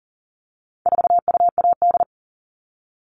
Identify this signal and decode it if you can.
Morse code, heard as a keyed tone, about 40 wpm, 700 Hz. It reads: 4UAD
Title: All Cut Numbers - 4-Digit 40wpm